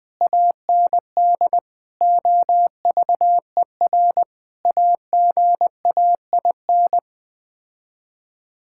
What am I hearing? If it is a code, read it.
Morse code, heard as a keyed tone, about 20 wpm, 700 Hz. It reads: AND OVER AGAIN
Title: Sets of 8 Words - 20wpm